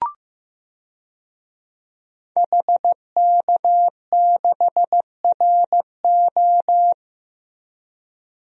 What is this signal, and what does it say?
Morse code, heard as a keyed tone, about 16 wpm, 700 Hz. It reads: HK6RO
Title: DXCC-100 Call Signs - 5 Characters - Repeated 3x - 15wpm